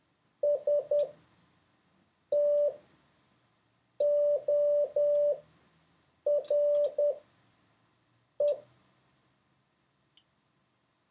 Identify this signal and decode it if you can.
Morse code, heard as a keyed tone, about 10 wpm, spaced as for 5 wpm, 580 Hz. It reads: STORE
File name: 4of6-words-1